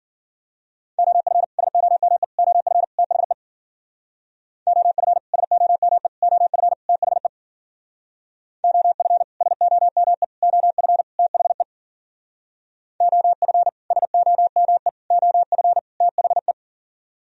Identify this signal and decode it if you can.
Morse code, heard as a keyed tone, about 51 wpm, 700 Hz. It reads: OF SOME OF THE OF SOME OF THE OF SOME OF THE OF SOME OF THE